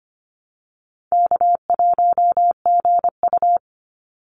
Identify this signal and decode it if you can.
Morse code, heard as a keyed tone, about 25 wpm, 700 Hz. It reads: K1GU
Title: Call Signs - 25wpm